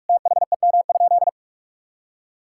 Morse code, heard as a keyed tone, about 42 wpm, 700 Hz. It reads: THEM?
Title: Sentences from Top 1000 Words - Review Part 5 - 45wpm